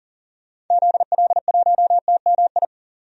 Morse code, heard as a keyed tone, about 40 wpm, 700 Hz. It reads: ZL1TMI
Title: DXCC-100 Call Signs - 6 Characters - 40wpm